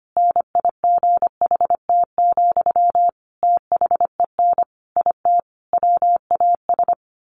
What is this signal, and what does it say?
Morse code, heard as a keyed tone, about 25 wpm, 700 Hz. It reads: NIGHT, THEN IT WAS
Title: A Sound Of Thunder 25wpm